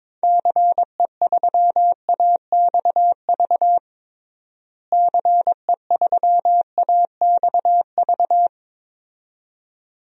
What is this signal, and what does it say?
Morse code, heard as a keyed tone, about 22 wpm, 700 Hz. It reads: CE3AXV CE3AXV
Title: DXCC-100 Call Signs - 6 Characters - Repeated 2x - 22wpm